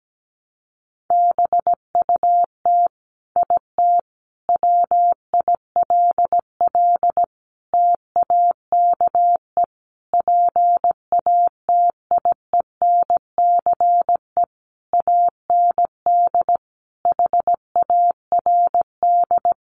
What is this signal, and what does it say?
Morse code, heard as a keyed tone, about 17 wpm, 700 Hz. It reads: BUT IT WILL TAKE PATIENCE AND HARD
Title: Sentences from Top 1300 Words 17wpm